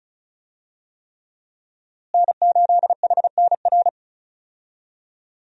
Morse code, heard as a keyed tone, about 34 wpm, 700 Hz. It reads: N8HNR
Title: USA Call Signs - 5 Characters - Repeated 3x - 35wpm